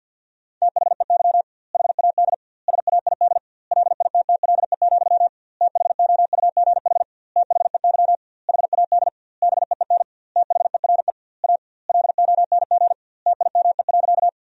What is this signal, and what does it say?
Morse code, heard as a keyed tone, about 50 wpm, 700 Hz. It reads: THEY HAD SAID LITTLE, THOUGH THEY HAD BEEN THERE A LONG TIME.
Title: 2x Word-Spacing - Sentences from Top 400 Words - 50wpm